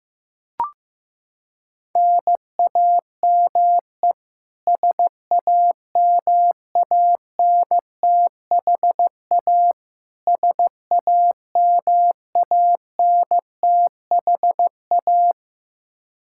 Morse code, heard as a keyed tone, about 15 wpm, 700 Hz. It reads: NAME SAMANTHA SAMANTHA
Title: Practice QSO - Exchange 1 - Beginning - 15wpm